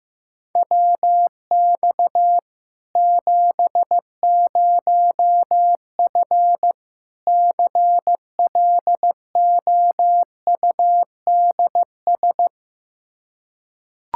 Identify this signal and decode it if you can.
Morse code, heard as a keyed tone, about 15 wpm, 700 Hz. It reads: WX 70F CLOUDS
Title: K1USN SST QSO Practice 15wpm